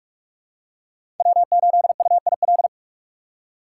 Morse code, heard as a keyed tone, about 43 wpm, 700 Hz. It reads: W8UIL
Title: K1USN SST QSO Practice - 45wpm